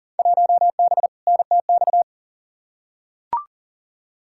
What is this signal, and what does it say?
Morse code, heard as a keyed tone, about 40 wpm, 700 Hz. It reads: 1B NTX
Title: Field Day Exchanges - 40wpm